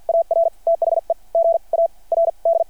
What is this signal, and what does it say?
Morse code, heard as a keyed tone, about 53 wpm, spaced as for 38 wpm, 600 Hz. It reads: AR THE GARD
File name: archived / CRSnet